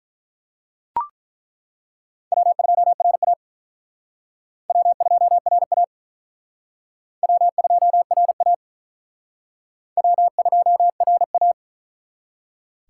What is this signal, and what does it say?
Morse code, heard as a keyed tone, about 42 wpm, 700 Hz. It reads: W2RA W2RA W2RA W2RA
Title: Speed Racing - USA 1x2 Call Signs 35wpm